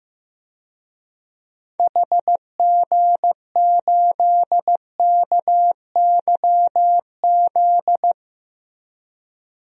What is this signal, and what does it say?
Morse code, heard as a keyed tone, about 15 wpm, 700 Hz. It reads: HG8KYZ
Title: DXCC-100 Call Signs - 6 Characters - Repeated 2x - 15wpm